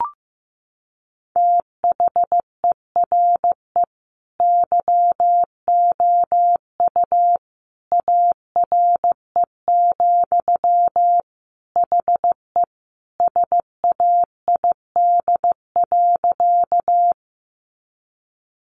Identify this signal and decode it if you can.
Morse code, heard as a keyed tone, about 15 wpm, 700 Hz. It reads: THERE YOU ARE, HE SAID.
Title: Sentences from Top 100 Words 15wpm